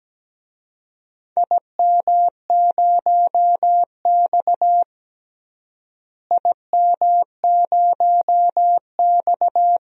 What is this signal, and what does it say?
Morse code, heard as a keyed tone, about 17 wpm, 700 Hz. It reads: IM0X IM0X
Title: DXCC-100 Call Signs - 4 Characters - Repeated 2x - 17wpm